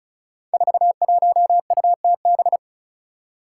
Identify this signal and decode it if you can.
Morse code, heard as a keyed tone, about 36 wpm, 700 Hz. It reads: 41UTB
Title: Five Challenging Letter-Number Combinations 35wpm